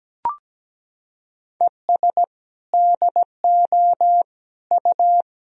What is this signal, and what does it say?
Morse code, heard as a keyed tone, about 17 wpm, 700 Hz. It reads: ES DO U
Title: Sentences from Top 100 Words and Abbreviations 17wpm